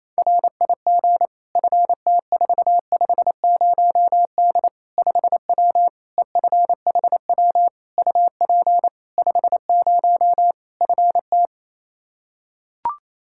Morse code, heard as a keyed tone, about 28 wpm, 700 Hz. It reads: RIG FT450D 5W EFHW UP 50 FT
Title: Practice QSO - Exchange 2 - Info - 28wpm